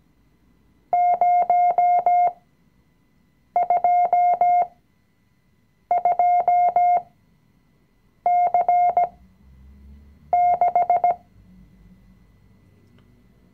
Morse code, heard as a keyed tone, about 17 wpm, spaced as for 6 wpm, 710 Hz. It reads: 022C6
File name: CW-Online-Kurs / Anfänger / Lektion 020